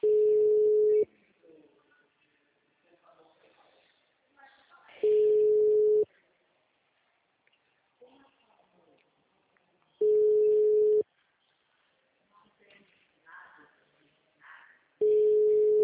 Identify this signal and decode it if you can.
Ringback tone (Europe, 425 Hz), with speech.
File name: out- -2040- - - .37